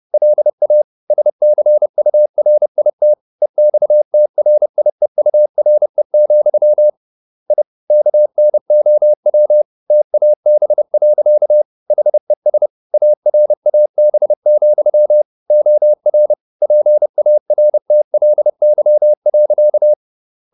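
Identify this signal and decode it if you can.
Morse code, heard as a keyed tone, about 30 wpm, 600 Hz. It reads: LA SCURIT EXTRIEURE, I KNOW TAB. HES ARAB, OR PARTLY.
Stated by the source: Morse code at 30 wpm, the novel Never by Ken Follett in English, from CW med Gnister.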